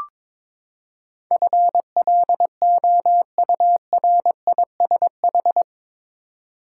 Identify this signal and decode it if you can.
Morse code, heard as a keyed tone, about 22 wpm, 700 Hz. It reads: FLOURISH
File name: practice / 22wpm / Eight Letter Words - 22wpm